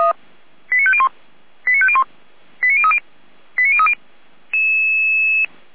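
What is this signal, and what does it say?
Five-tone selective call (ZVEI), heard as quick runs of five tones, tones of about 70 ms. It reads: 89501 89501 89030 89030
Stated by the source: ZVEI five-tone call received by SDR, server of Feuerwehr Orlamünde (fire brigade).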